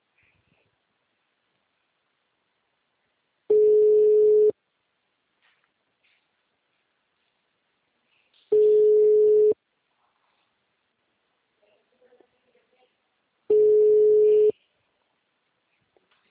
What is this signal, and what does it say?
Ringback tone (Europe, 425 Hz), with speech.
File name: out- -2013